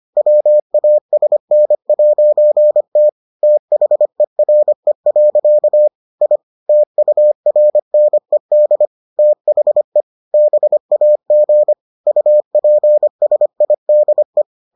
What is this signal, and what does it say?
Morse code, heard as a keyed tone, about 25 wpm, 600 Hz. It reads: WASN'T THERE. I TURNED THE BAG UPSIDE